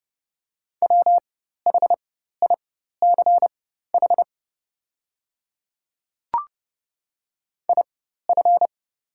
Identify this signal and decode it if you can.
Morse code, heard as a keyed tone, about 29 wpm, spaced as for 14 wpm, 700 Hz. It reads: WHICH IF
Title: 4x Character Spacing - Top 100 Words - 30wpm